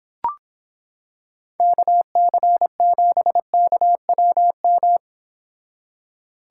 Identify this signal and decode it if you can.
Morse code, heard as a keyed tone, about 26 wpm, 700 Hz. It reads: KC7KWM